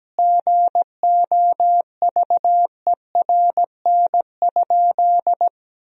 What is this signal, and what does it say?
Morse code, heard as a keyed tone, about 17 wpm, 700 Hz. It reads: GOVERN?
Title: Sentences from Top 1000 Words - Review Part 2 - 17wpm